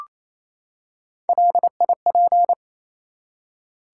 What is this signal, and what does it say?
Morse code, heard as a keyed tone, about 28 wpm, 700 Hz. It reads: LIP
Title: Three Letter Words - 28wpm